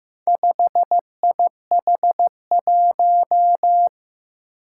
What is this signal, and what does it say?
Morse code, heard as a keyed tone, about 15 wpm, 700 Hz. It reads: 5IH1
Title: Four Challenging Letter-Number Combinations 15wpm